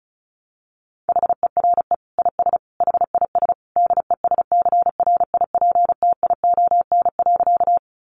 Morse code, heard as a keyed tone, about 34 wpm, 700 Hz. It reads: HERE IS HIS DESCRIPTION.
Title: Sentences from Top 1300 Words 35wpm